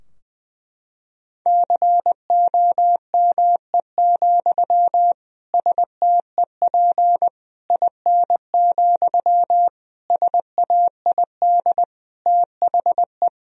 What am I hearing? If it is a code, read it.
Morse code, heard as a keyed tone, about 20 wpm, 700 Hz. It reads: COME, STEP IN, SAID THE
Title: Sentences from Top 1000 Words - Review Part 2 - 20wpm